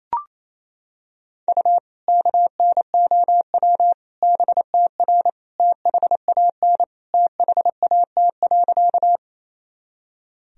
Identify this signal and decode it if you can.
Morse code, heard as a keyed tone, about 28 wpm, 700 Hz. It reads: U KNOW BTR THAN THAT.
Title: Sentences from Top 400 Words and Abbreviations 28wpm